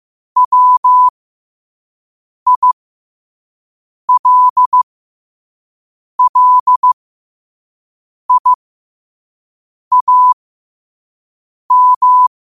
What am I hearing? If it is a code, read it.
Morse code, heard as a keyed tone, about 15 wpm, spaced as for 5 wpm, 1000 Hz. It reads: WILLIAM
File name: qotdfiles / 5wpm